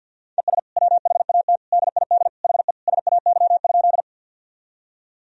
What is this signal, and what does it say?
Morse code, heard as a keyed tone, about 50 wpm, 700 Hz. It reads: ES WHAT DID HE SAY?
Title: Sentences from Top 200 Words and Abbreviations 50wpm